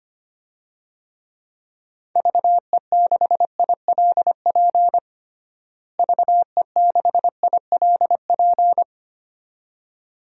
Morse code, heard as a keyed tone, about 25 wpm, 700 Hz. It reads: VE6ILP VE6ILP
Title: DXCC-100 Call Signs - 6 Characters - Repeated 2x - 25wpm